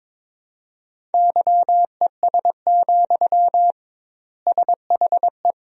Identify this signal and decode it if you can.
Morse code, heard as a keyed tone, about 22 wpm, 700 Hz. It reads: YES, SHE
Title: 2x Word-Spacing - Sentences from Top 500 Words - 22wpm